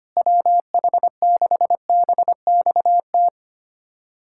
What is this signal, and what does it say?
Morse code, heard as a keyed tone, about 25 wpm, 700 Hz. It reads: WH6BXT
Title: USA Callsign DE Callsign 25wpm